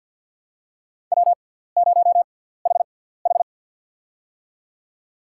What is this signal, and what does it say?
Morse code, heard as a keyed tone, about 49 wpm, spaced as for 17 wpm, 700 Hz. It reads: W0HH
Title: USA Call Signs - 6x Character Spacing - 50wpm